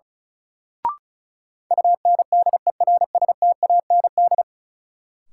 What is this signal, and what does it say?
Morse code, heard as a keyed tone, about 36 wpm, 700 Hz. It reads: UNDERSTAND
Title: Sentences from Top 600 Words - New Word Drill - 35wpm